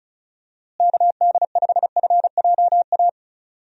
Morse code, heard as a keyed tone, about 34 wpm, 700 Hz. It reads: KD5FJA
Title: USA Call Signs - 6 Characters - Repeated 2x - 35wpm